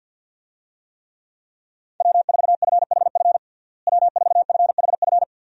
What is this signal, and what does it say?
Morse code, heard as a keyed tone, about 51 wpm, 700 Hz. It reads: W4FHF W4FHF
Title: USA Call Signs - 5 Characters - Repeated 3x - 50wpm